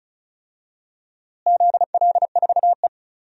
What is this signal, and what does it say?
Morse code, heard as a keyed tone, about 36 wpm, 700 Hz. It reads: ZL4E
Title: DXCC-100 Call Signs - 4 Characters - Repeated 2x - 35wpm